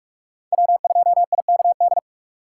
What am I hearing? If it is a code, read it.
Morse code, heard as a keyed tone, about 47 wpm, 700 Hz. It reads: W2IKD
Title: USA Call Signs - 6x Character Spacing - 45wpm